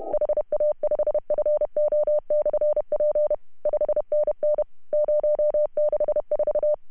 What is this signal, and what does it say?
Morse code, heard as a keyed tone, about 31 wpm, 600 Hz. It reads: HA5FO/P 5NN 064
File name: CW / HA2OS RHOB 2017 CW